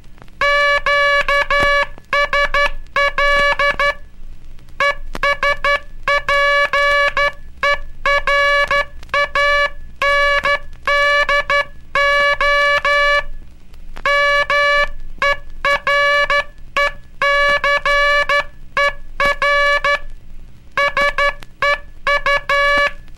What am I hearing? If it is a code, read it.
Morse code, heard as a keyed tone, about 12 wpm, 1700 Hz. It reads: QSL ESPERANDO MERECER SEU
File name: cursos / Curso CW mp3 / CW Aula - 18